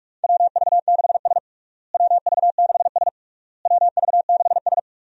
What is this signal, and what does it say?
Morse code, heard as a keyed tone, about 46 wpm, 700 Hz. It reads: WV6S WV6S WV6S
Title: USA Call Signs - 4 Characters - Repeated 3x - 45wpm